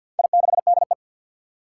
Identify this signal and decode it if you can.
Morse code, heard as a keyed tone, about 46 wpm, 700 Hz. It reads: I6DE
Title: Four Challenging Letter-Number Combinations 50wpm